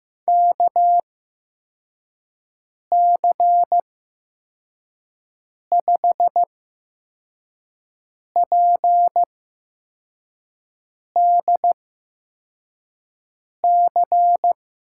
Morse code, heard as a keyed tone, about 15 wpm, spaced as for 4 wpm, 700 Hz. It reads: KC5PDC
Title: USA Call Signs - 6 Characters - 8x Character Spacing - 15wpm